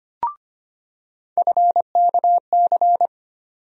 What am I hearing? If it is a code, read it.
Morse code, heard as a keyed tone, about 25 wpm, 700 Hz. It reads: FKC
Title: USA Call Sign Postfixes - 3 Letters 25wpm